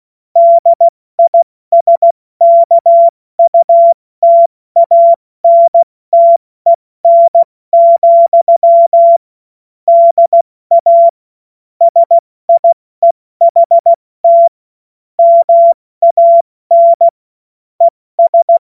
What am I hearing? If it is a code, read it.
Morse code, heard as a keyed tone, about 16 wpm, spaced as for 14 wpm, 680 Hz. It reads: DISKUTANTEN, DA SIEHT MAN ES
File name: Morsekurs / Witze / Witz 16. Dezember 2025 70 F80